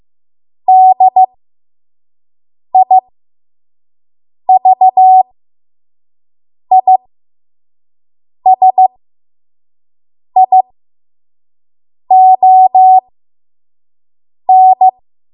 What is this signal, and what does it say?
Morse code, heard as a keyed tone, about 15 wpm, spaced as for 5 wpm, 750 Hz. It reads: DIVISION
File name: Morse / 05WPM